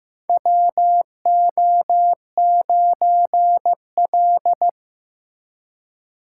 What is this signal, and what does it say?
Morse code, heard as a keyed tone, about 15 wpm, 700 Hz. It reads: WO9L
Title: USA Call Signs - 4 Characters - 2x Character Spacing - 15wpm